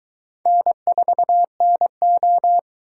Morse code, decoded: N4NO

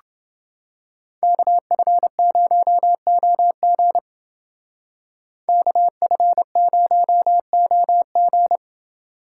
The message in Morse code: KF0OG KF0OG

